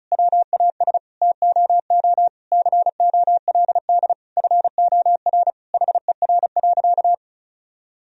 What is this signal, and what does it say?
Morse code, heard as a keyed tone, about 36 wpm, 700 Hz. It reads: WAS TOO COLD FOR HER.
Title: Sentences from Top 1000 Words - Review Part 4 - 35wpm